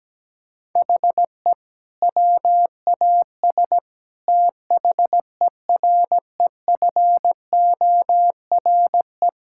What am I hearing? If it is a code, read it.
Morse code, heard as a keyed tone, about 17 wpm, 700 Hz. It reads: HE WAS THEREFORE
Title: Sentences from Top 1000 Words - Review Part 5 - 17wpm